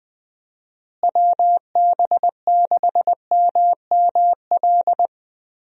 Morse code, heard as a keyed tone, about 20 wpm, 700 Hz. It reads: WB6MML
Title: USA Call Signs - 6 Characters - Repeated 3x - 20wpm